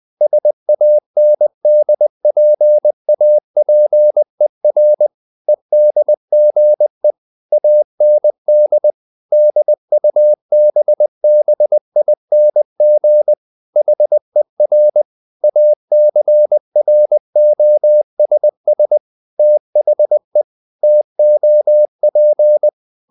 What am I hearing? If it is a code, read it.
Morse code, heard as a keyed tone, about 20 wpm, 600 Hz. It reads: SANDPAPER EDGE AND DUBBING HER ACROSS THE TOP
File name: the old man and the sea / the old man and the sea part 1 20wpm